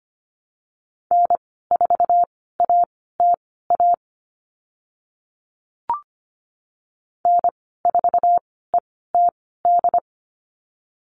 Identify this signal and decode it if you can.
Morse code, heard as a keyed tone, about 25 wpm, spaced as for 16 wpm, 700 Hz. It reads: N4ATA N4ETD
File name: Lesson 15g - Number 5 - Call Signs 25wpm